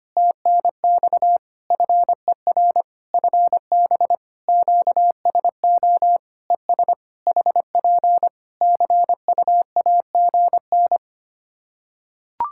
Morse code, heard as a keyed tone, about 25 wpm, 700 Hz. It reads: TNX FER FB QSO ES HP CUAGN
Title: K1USN SST QSO Practice 25wpm